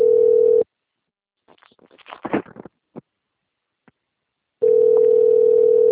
Ringback tone (North America), 440 + 480 Hz.